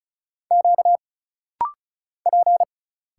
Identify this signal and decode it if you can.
Morse code, heard as a keyed tone, about 34 wpm, 700 Hz. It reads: Q P
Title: Single Letters - Rapid-Fire 35wpm